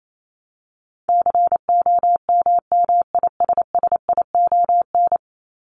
Morse code, heard as a keyed tone, about 28 wpm, 700 Hz. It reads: COMMISSION
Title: Top 1251-1500 Words 28wpm